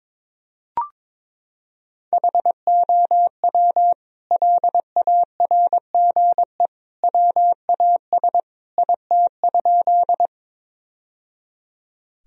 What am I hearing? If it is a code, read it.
Morse code, heard as a keyed tone, about 21 wpm, 700 Hz. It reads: HOW LARGE WAS IT?
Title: Sentences from Top 1000 Words - Review Part 1 - 22wpm